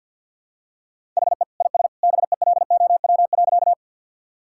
Morse code, heard as a keyed tone, about 50 wpm, 700 Hz. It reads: HE IS BELOW.
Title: Sentences from Top 1000 Words - Review Part 5 - 50wpm